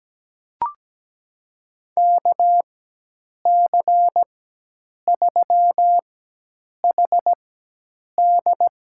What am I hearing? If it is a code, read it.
Morse code, heard as a keyed tone, about 17 wpm, spaced as for 8 wpm, 700 Hz. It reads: KC3HD
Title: USA Call Signs - 6 Characters - 4x Character Spacing - 17wpm